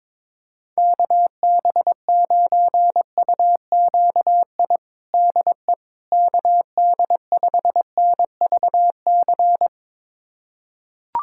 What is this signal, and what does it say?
Morse code, heard as a keyed tone, about 22 wpm, 700 Hz. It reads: KB9UQI DE KD5NVC
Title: USA Callsign DE Callsign - 6 Characters - 22wpm